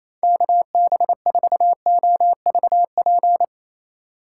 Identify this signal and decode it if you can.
Morse code, heard as a keyed tone, about 28 wpm, 700 Hz. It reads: KB4OVP
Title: USA Call Signs - 2x Character Spacing - 28wpm